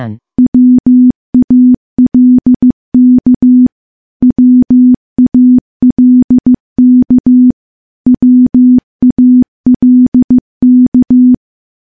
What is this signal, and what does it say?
Morse code, heard as a keyed tone, about 15 wpm, 250 Hz. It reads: WALK WALK WALK
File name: morse-training-15WPM